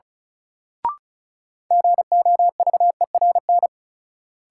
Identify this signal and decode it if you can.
Morse code, heard as a keyed tone, about 36 wpm, 700 Hz. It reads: GOVERN
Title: Sentences from Top 1000 Words - Review Part 2 - New Word Drill - 35wpm